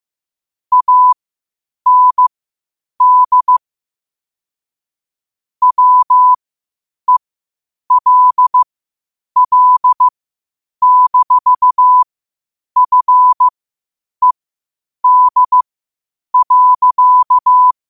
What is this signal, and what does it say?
Morse code, heard as a keyed tone, about 15 wpm, spaced as for 8 wpm, 1000 Hz. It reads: AND WELL-FED.
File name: qotdfiles / 8wpm